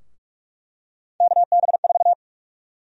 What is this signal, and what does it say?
Morse code, heard as a keyed tone, about 46 wpm, 700 Hz. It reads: KB4
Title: USA Call Sign Prefixes - Repeated 2x - 45wpm